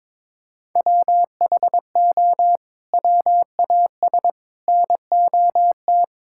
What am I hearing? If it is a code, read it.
Morse code, heard as a keyed tone, about 22 wpm, 700 Hz. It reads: WHO WAS NOT
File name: practice / 22wpm / Sets of 8 Words - 22wpm